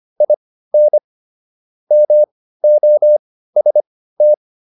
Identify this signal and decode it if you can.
Morse code, heard as a keyed tone, about 25 wpm, spaced as for 15 wpm, 600 Hz. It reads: IN MOST